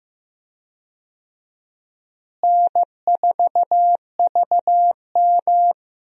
Morse code, heard as a keyed tone, about 15 wpm, 700 Hz. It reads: N4VM
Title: USA Call Signs - 4 Characters - Repeated 3x - 15wpm